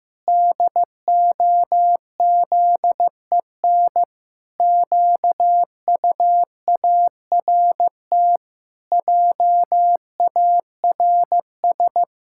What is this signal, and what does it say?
Morse code, heard as a keyed tone, about 15 wpm, 700 Hz. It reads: DOZEN QUART JARS
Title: Pangram Challenge v2 - 15wpm